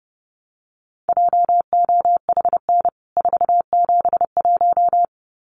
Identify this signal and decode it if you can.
Morse code, heard as a keyed tone, about 30 wpm, 700 Hz. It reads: JOHN 471
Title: CWT Exchange 30wpm